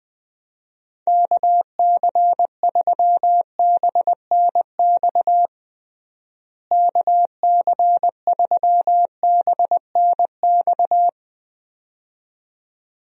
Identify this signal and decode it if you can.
Morse code, heard as a keyed tone, about 20 wpm, 700 Hz. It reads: KC3BNX KC3BNX
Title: USA Call Signs - 6 Characters - Repeated 2x - 20wpm